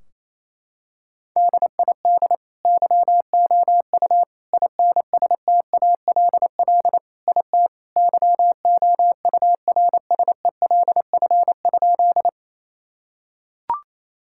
Morse code, heard as a keyed tone, about 27 wpm, 700 Hz. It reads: DID YOU INSTALL IT YOURSELF?
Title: Sentences from Top 1600 Words 28wpm